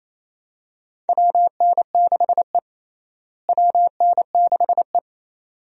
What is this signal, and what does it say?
Morse code, heard as a keyed tone, about 28 wpm, 700 Hz. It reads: WN6E WN6E
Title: USA Call Signs - 4 Characters - Repeated 2x - 28wpm